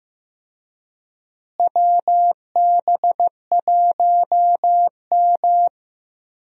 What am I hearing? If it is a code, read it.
Morse code, heard as a keyed tone, about 16 wpm, 700 Hz. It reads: WB1M
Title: USA Call Signs - 4 Characters - Repeated 2x - 15wpm